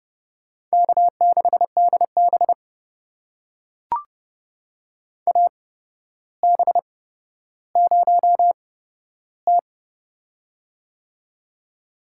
Morse code, heard as a keyed tone, about 31 wpm, spaced as for 13 wpm, 700 Hz. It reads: K6DB AB0T
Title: USA Call Signs - 4 Characters - 8x Character Spacing - 30wpm